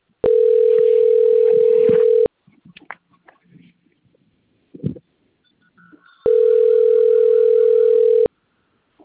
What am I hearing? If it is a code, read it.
Ringback tone (North America), 440 + 480 Hz.